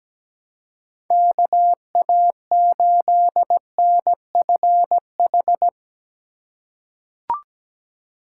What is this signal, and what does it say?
Morse code, heard as a keyed tone, about 17 wpm, 700 Hz. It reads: KA8NFH
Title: USA Call Signs - 4x Character Spacing - 17wpm